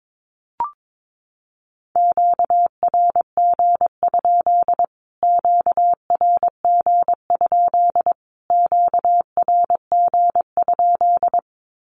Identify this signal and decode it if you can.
Morse code, heard as a keyed tone, about 22 wpm, 700 Hz. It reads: QRG? QRG? QRG?
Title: Q-Codes Repeated 3x 22wpm